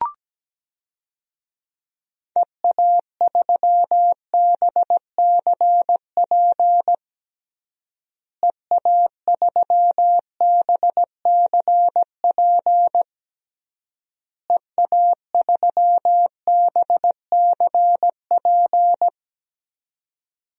Morse code, heard as a keyed tone, about 17 wpm, 700 Hz. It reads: EA3BCP EA3BCP EA3BCP
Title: DXCC-100 Call Signs - 6 Characters - Repeated 3x - 17wpm